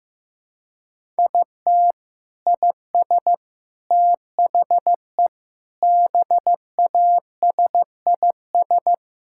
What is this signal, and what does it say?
Morse code, heard as a keyed tone, about 15 wpm, 700 Hz. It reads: IT IS THE BASIS
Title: Sentences from Top 1000 Words - Review Part 5 - 15wpm